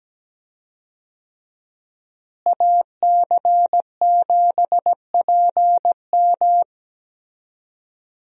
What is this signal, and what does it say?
Morse code, heard as a keyed tone, about 17 wpm, 700 Hz. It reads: AC7PM
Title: USA Call Signs - 5 Characters - Repeated 2x - 17wpm